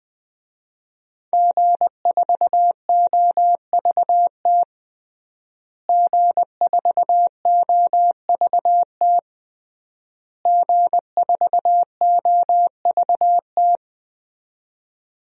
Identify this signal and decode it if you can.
Morse code, heard as a keyed tone, about 20 wpm, 700 Hz. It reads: G4OVT G4OVT G4OVT
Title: DXCC-100 Call Signs - 5 Characters - Repeated 3x - 20wpm